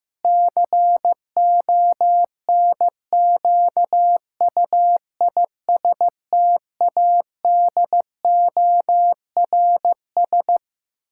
Morse code, heard as a keyed tone, about 15 wpm, 700 Hz. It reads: CONQUISTADORS
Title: Pangram Challenge v2 - 15wpm